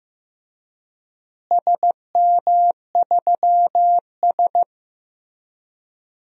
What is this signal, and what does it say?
Morse code, heard as a keyed tone, about 15 wpm, 700 Hz. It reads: SM3S
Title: DXCC-100 Call Signs - 4 Characters - Repeated 3x - 15wpm